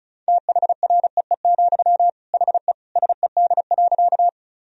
Morse code, heard as a keyed tone, about 34 wpm, 700 Hz. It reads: THREE, HE SED.
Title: Sentences from Top 200 Words and Abbreviations 35wpm